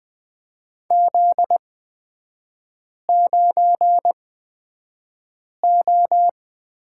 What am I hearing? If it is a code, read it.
Morse code, heard as a keyed tone, about 20 wpm, spaced as for 5 wpm, 700 Hz. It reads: Z9O
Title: 243 Lesson 39e - Letter z - Three Characters 20wpm